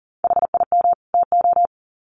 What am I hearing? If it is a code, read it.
Morse code, heard as a keyed tone, about 40 wpm, 700 Hz. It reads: HIM TO
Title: Sets of 2 Words 40wpm